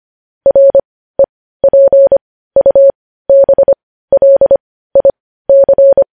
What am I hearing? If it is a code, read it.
Morse code, heard as a keyed tone, about 25 wpm, spaced as for 15 wpm, 550 Hz. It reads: REPUBLIC
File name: podcast / News headlines-1x-25x15